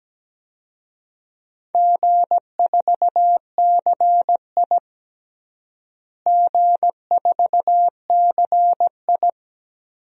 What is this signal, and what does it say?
Morse code, heard as a keyed tone, about 17 wpm, 700 Hz. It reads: G4CI G4CI
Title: DXCC-100 Call Signs - 4 Characters - Repeated 2x - 17wpm